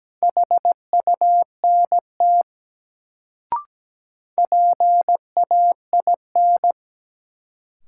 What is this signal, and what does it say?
Morse code, heard as a keyed tone, about 17 wpm, 700 Hz. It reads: HUNT PAIN
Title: Sentences from Top 1600 Words - New Word Drill Alt - 17wpm